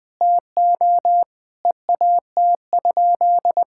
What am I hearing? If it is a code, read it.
Morse code, heard as a keyed tone, about 21 wpm, 700 Hz. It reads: TO EAT?
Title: Sentences with 5-Word Phrases - Spoken Beginning - 20wpm